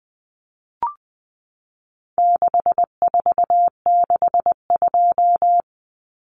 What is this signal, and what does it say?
Morse code, heard as a keyed tone, about 20 wpm, 700 Hz. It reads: 6462
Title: Numbers - 4-Digit 20wpm